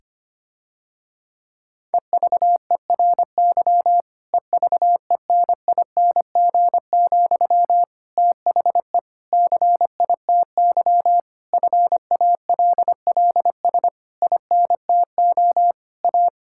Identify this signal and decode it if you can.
Morse code, heard as a keyed tone, about 25 wpm, 700 Hz. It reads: EVERY EVENING, THE CITY FALLS INTO A
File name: practice / 25wpm / Difficult Sentences with Hints - 25wpm